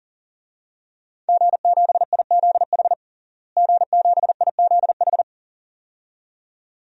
Morse code, heard as a keyed tone, about 40 wpm, 700 Hz. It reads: G7IZH G7IZH
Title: DXCC-100 Call Signs - 5 Characters - Repeated 2x - 40wpm